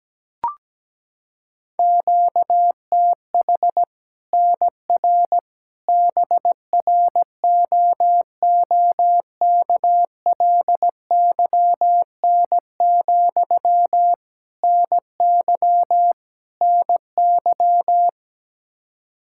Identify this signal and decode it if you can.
Morse code, heard as a keyed tone, about 17 wpm, 700 Hz. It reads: QTH NR BROOKLYN, NY NY
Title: Practice QSO - Exchange 1 - Beginning - 17wpm